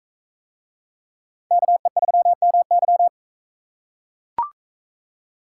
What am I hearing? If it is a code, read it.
Morse code, heard as a keyed tone, about 41 wpm, 700 Hz. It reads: KE3MY